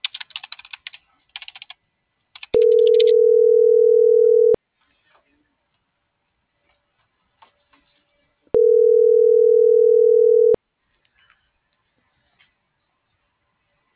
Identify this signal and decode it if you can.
Ringback tone (North America), 440 + 480 Hz.